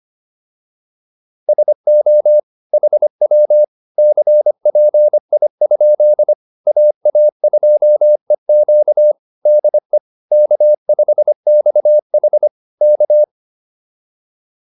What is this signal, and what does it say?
Morse code, heard as a keyed tone, about 25 wpm, 600 Hz. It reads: SO HW CPI? AA2EQ DE K5XH K
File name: QSO210a 25